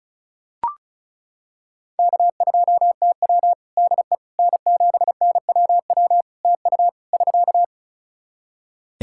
Morse code, heard as a keyed tone, about 36 wpm, 700 Hz. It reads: K2TW DE N7NWW TU <SK>
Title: K1USN SST QSO Practice 35wpm